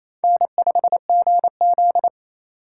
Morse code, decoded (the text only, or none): N5GZ